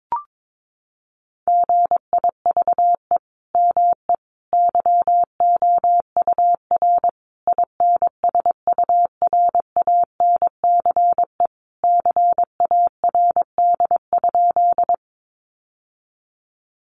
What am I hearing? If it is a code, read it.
Morse code, heard as a keyed tone, about 22 wpm, 700 Hz. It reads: GIVE ME YOUR INSURANCE CARD?
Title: Sentences from Top 1100 Words 22wpm